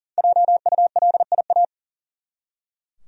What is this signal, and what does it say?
Morse code, heard as a keyed tone, about 40 wpm, 700 Hz. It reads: JULIA
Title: USA Common Names - Max 5 Letters - 40wpm